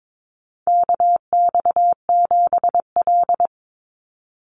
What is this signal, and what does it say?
Morse code, heard as a keyed tone, about 22 wpm, 700 Hz. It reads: KX7L
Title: NA SOTA Chaser Callsigns 22wpm